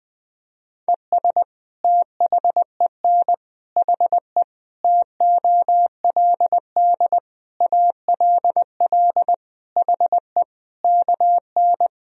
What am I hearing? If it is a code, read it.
Morse code, heard as a keyed tone, about 21 wpm, 700 Hz. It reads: ES THEN HE TOLD ALL HE KN
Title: Sentences from Top 400 Words and Abbreviations 20wpm